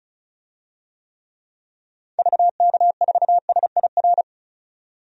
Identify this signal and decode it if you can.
Morse code, heard as a keyed tone, about 34 wpm, 700 Hz. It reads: VK4SIR
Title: DXCC-100 Call Signs - 6 Characters - 35wpm